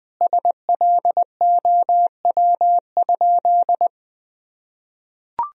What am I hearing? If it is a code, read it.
Morse code, heard as a keyed tone, about 20 wpm, 700 Hz. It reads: SLOW?
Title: Sentences from Top 1000 Words - Review Part 2 - 20wpm